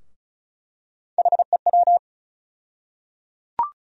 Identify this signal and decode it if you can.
Morse code, heard as a keyed tone, about 34 wpm, 700 Hz. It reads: HEW
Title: USA Call Sign Postfixes - 3 Letters 35wpm